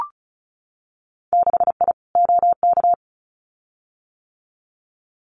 Morse code, heard as a keyed tone, about 36 wpm, 700 Hz. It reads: 6I OK
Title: Winter Field Day Exchanges Encore 35wpm